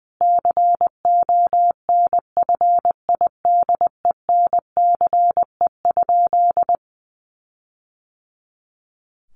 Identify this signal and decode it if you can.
Morse code, heard as a keyed tone, about 20 wpm, 700 Hz. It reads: CONFIDENCE?
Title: Sentences from Top 1500 Words 20wpm